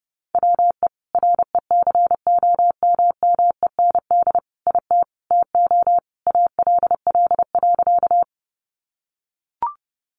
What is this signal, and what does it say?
Morse code, heard as a keyed tone, about 30 wpm, 700 Hz. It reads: WE RECOMMEND IT TO ALL.
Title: Sentences from Top 1200 Words 30wpm